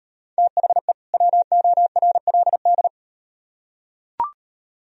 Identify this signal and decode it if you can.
Morse code, heard as a keyed tone, about 38 wpm, 700 Hz. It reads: THE WORLD